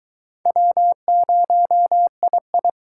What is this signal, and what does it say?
Morse code, heard as a keyed tone, about 20 wpm, 700 Hz. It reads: W0II